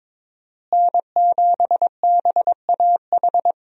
Morse code, heard as a keyed tone, about 22 wpm, 700 Hz. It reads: N7BAH
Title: USA Call Signs - 5 Characters - Repeated 2x - 22wpm